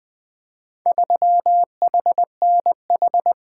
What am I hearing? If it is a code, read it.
Morse code, heard as a keyed tone, about 20 wpm, 700 Hz. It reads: 3HNH
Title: Four Letter-Number Combinations - 20wpm